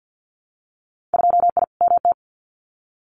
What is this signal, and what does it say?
Morse code, heard as a keyed tone, about 50 wpm, 700 Hz. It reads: 2I MT
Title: Winter Field Day Exchanges Encore 50wpm